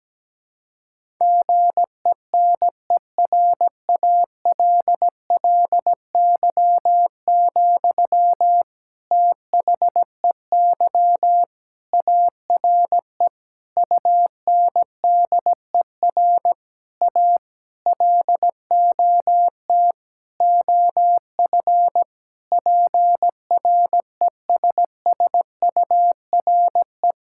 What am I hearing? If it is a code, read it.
Morse code, heard as a keyed tone, about 17 wpm, 700 Hz. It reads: GENERALLY, THEY ARE UNDER A LOT OF PRESSURE
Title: Sentences from Top 1000 Words - Review Part 6 - 17wpm